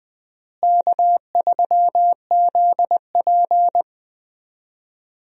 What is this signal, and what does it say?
Morse code, heard as a keyed tone, about 20 wpm, 700 Hz. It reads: K3ZP
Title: USA Call Signs - 4 Characters - Repeated 2x - 20wpm